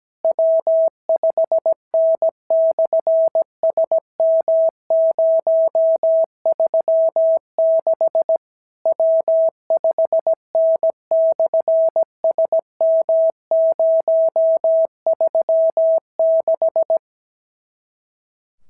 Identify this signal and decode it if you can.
Morse code, heard as a keyed tone, about 17 wpm, 650 Hz. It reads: W5N/SM036 W5N/SM036
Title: SOTA QSO Practice 17wpm